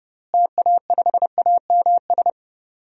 Morse code, heard as a keyed tone, about 32 wpm, 700 Hz. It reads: TA5AMS
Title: DXCC-100 Call Signs - 6 Characters - Repeated 3x - 30wpm